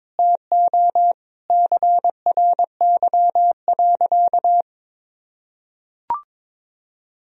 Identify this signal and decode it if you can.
Morse code, heard as a keyed tone, about 22 wpm, 700 Hz. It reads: TO CRY.
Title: Sentences from Top 1000 Words - Review Part 2 - 22wpm